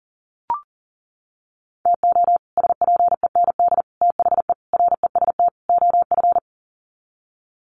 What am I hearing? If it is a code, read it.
Morse code, heard as a keyed tone, about 40 wpm, 700 Hz. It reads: TO SPEND THE REST OF
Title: Sets of 5 Words Encore2 40wpm